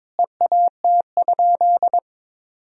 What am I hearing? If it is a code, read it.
Morse code, heard as a keyed tone, about 22 wpm, 700 Hz. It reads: EAT?
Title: Sentences from Top 300 Words and Abbreviations 22wpm